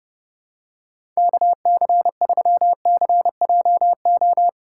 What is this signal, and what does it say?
Morse code, heard as a keyed tone, about 30 wpm, 700 Hz. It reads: KC3CJO
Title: USA Call Signs - 6 Characters - 3x Character Spacing - 30wpm